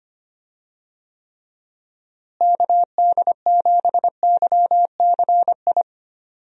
Morse code, heard as a keyed tone, about 25 wpm, 700 Hz. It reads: KD7YCI
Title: USA Call Signs - 6 Characters - Repeated 2x - 25wpm